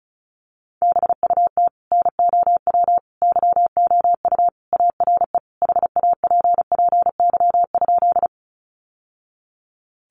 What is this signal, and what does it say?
Morse code, heard as a keyed tone, about 34 wpm, 700 Hz. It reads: BUT NOW YOU ARE HAPPY?
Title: Sentences from Top 1000 Words 35wpm